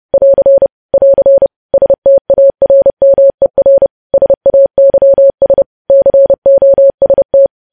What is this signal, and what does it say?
Morse code, heard as a keyed tone, about 30 wpm, 550 Hz. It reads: <AR> <AR> STARMER SAYS COST
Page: news headlines in morse code at 30 wpm